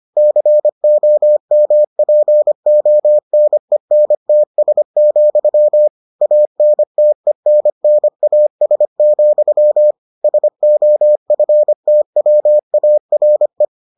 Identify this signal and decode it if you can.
Morse code, heard as a keyed tone, about 25 wpm, 600 Hz. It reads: COMPONENTS, ANTENNAS, SOFTWARE